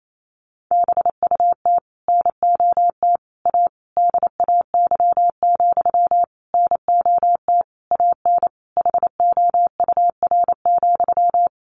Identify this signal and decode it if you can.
Morse code, heard as a keyed tone, about 28 wpm, 700 Hz. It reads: BUT NOT A DAY, NOT AN HOUR,
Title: Sentences from Top 1400 Words 28wpm